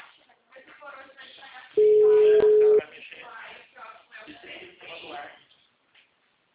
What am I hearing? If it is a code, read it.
Ringback tone (Europe, 425 Hz), with speech.
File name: out- -2006- - - .75260